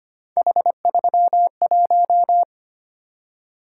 Morse code, heard as a keyed tone, about 25 wpm, 700 Hz. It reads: H31
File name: Three Letter-Number Combinations - 25wpm